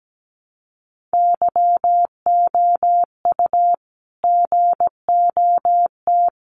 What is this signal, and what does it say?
Morse code, heard as a keyed tone, about 17 wpm, 700 Hz. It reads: YOU GOT
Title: Sentences from Top 300 Words 17wpm